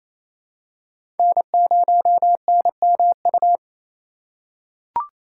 Morse code, heard as a keyed tone, about 28 wpm, 700 Hz. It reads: N0NMU
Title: USA Call Signs - 5 Characters - 2x Character Spacing - 28wpm